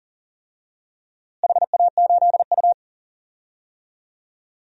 Morse code, heard as a keyed tone, about 40 wpm, 700 Hz. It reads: HA8U
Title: DXCC-100 Call Signs - 4 Characters - Repeated 3x - 40wpm